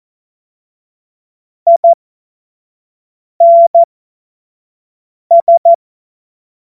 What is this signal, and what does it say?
Morse code, heard as a keyed tone, about 14 wpm, spaced as for 5 wpm, 680 Hz. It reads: INS
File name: Morsekurs / Witze / Witz 31. Januar 2023 25 F70